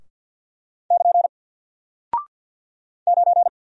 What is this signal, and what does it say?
Morse code, heard as a keyed tone, about 48 wpm, 700 Hz. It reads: <KN> 9
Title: Single Character (Letter:Number:Punctuation:Prosign) Alt 50wpm